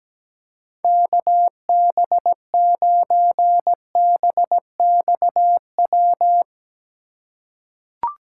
Morse code, heard as a keyed tone, about 17 wpm, 700 Hz. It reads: KB9BXW
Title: USA Call Signs - 6 Characters - 3x Character Spacing - 17wpm